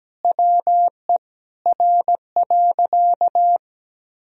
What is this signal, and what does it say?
Morse code, heard as a keyed tone, about 17 wpm, 700 Hz. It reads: WE R.
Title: Sentences from Top 200 Words and Abbreviations 17wpm